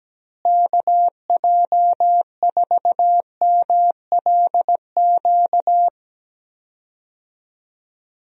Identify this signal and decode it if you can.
Morse code, heard as a keyed tone, about 17 wpm, 700 Hz. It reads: KJ4MLQ
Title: USA Call Signs - 6 Characters - Repeated 3x - 17wpm